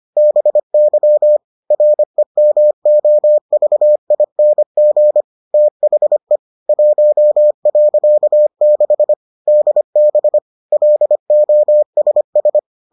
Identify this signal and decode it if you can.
Morse code, heard as a keyed tone, about 25 wpm, 600 Hz. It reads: BY REMOVING THE 1.6 DB LOSS